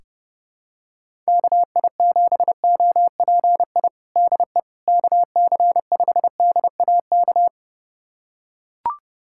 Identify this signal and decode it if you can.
Morse code, heard as a keyed tone, about 30 wpm, 700 Hz. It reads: KI7OPI DE KC5DAK
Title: USA Callsign DE Callsign 30wpm